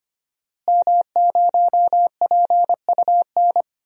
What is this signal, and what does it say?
Morse code, heard as a keyed tone, about 25 wpm, 700 Hz. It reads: M0PUN
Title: DXCC-100 Call Signs - 5 Characters - Repeated 2x - 25wpm